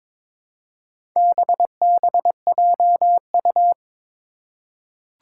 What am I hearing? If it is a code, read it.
Morse code, heard as a keyed tone, about 22 wpm, 700 Hz. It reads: BBJU
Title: Four Challenging Letter-Number Combinations 22wpm